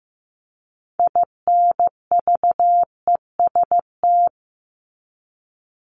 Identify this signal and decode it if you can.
Morse code, heard as a keyed tone, about 15 wpm, 700 Hz. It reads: INVEST
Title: Top 1001-1250 Words 15wpm